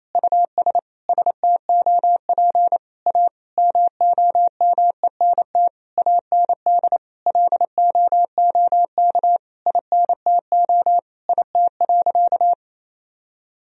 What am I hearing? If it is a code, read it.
Morse code, heard as a keyed tone, about 28 wpm, 700 Hz. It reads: US STOP A MOMENT AND LOOK INTO IT.
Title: Sentences from Top 1000 Words - Review Part 1 - 28wpm